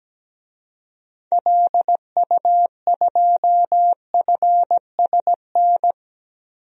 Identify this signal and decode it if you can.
Morse code, heard as a keyed tone, about 17 wpm, 700 Hz. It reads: LU2FSN
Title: DXCC-100 Call Signs - 6 Characters - 17wpm